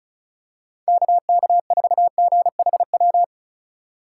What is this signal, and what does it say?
Morse code, heard as a keyed tone, about 36 wpm, 700 Hz. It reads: KK4GHW
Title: USA Call Signs - 6 Characters - 2x Character Spacing - 35wpm